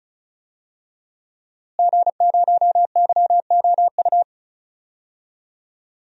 Morse code, heard as a keyed tone, about 36 wpm, 700 Hz. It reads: G0YOU
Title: DXCC-100 Call Signs - 5 Characters - Repeated 2x - 35wpm